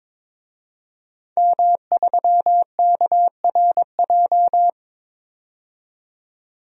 Morse code, heard as a keyed tone, about 22 wpm, 700 Hz. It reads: M3KRJ
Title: DXCC-100 Call Signs - 5 Characters - 22wpm